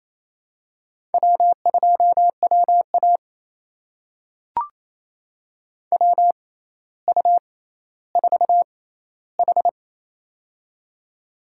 Morse code, heard as a keyed tone, about 27 wpm, spaced as for 14 wpm, 700 Hz. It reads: W2WA WU4H
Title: USA Call Signs - 4 Characters - 6x Character Spacing - 28wpm